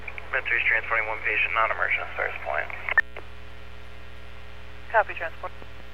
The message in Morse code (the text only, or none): none